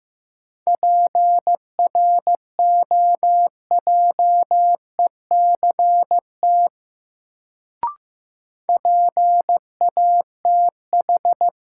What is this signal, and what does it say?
Morse code, heard as a keyed tone, about 15 wpm, 700 Hz. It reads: PROJECT PATH
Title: Sentences from Top 1400 Words - New Word Drill Alt - 15wpm